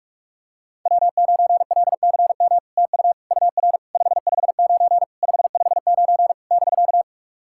Morse code, heard as a keyed tone, about 44 wpm, 660 Hz. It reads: W9LCM TU UR 559 559 <BK>
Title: SOTA QSO Practice 45wpm